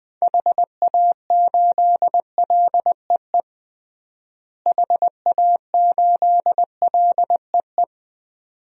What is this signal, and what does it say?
Morse code, heard as a keyed tone, about 20 wpm, 700 Hz. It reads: HA8LEE HA8LEE
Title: DXCC-100 Call Signs - 6 Characters - Repeated 2x - 20wpm